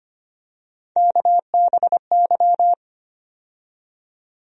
Morse code, heard as a keyed tone, about 25 wpm, 700 Hz. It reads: KBY
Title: USA Call Sign Postfixes - 3 Letters 25wpm